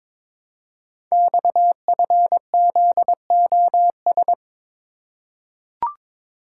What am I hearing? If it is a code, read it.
Morse code, heard as a keyed tone, about 22 wpm, 700 Hz. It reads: XFZOS
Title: Code Groups - 5 Letters - 22wpm